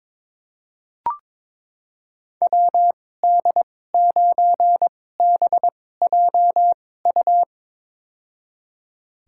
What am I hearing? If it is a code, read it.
Morse code, heard as a keyed tone, about 22 wpm, spaced as for 16 wpm, 700 Hz. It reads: WD9BJU
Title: USA Call Signs - 6 Characters - 2x Character Spacing - 22wpm